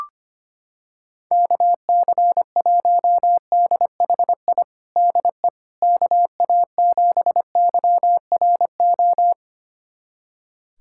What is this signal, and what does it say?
Morse code, heard as a keyed tone, about 25 wpm, 700 Hz. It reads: KC1DHI DE KA7YRO
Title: USA Callsign DE Callsign - 6 Characters - 25wpm